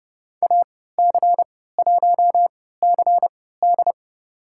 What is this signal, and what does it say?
Morse code, heard as a keyed tone, about 32 wpm, spaced as for 17 wpm, 700 Hz. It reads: AC1CD
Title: USA Call Signs - 5 Characters - 3x Character Spacing - 30wpm